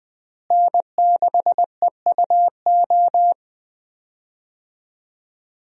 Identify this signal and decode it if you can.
Morse code, heard as a keyed tone, about 20 wpm, 700 Hz. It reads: N6EUO
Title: USA Call Signs - 5 Characters - Repeated 3x - 20wpm